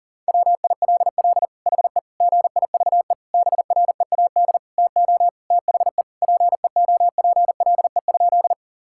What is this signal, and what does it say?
Morse code, heard as a keyed tone, about 40 wpm, 700 Hz. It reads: WILL HE GIVE BREAD TO THE PEOPLE?
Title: Sentences from Top 1600 Words 40wpm